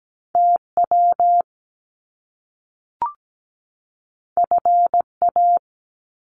Morse code, heard as a keyed tone, about 17 wpm, 700 Hz. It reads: TW FA
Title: Top Two Letter Combinations in Words 17wpm